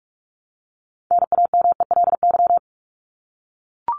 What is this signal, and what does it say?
Morse code, heard as a keyed tone, about 44 wpm, 700 Hz. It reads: NAMELY
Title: LY Suffix - Top 100 Words with Base Words 45wpm